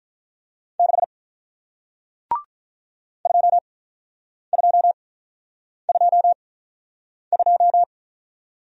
Morse code, heard as a keyed tone, about 48 wpm, spaced as for 9 wpm, 700 Hz. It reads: 6 2222